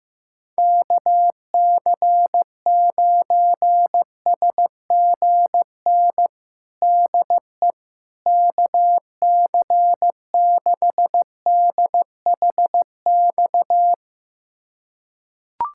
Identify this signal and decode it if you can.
Morse code, heard as a keyed tone, about 15 wpm, 700 Hz. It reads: KC9SGN DE KC6DHX
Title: USA Callsign DE Callsign 15wpm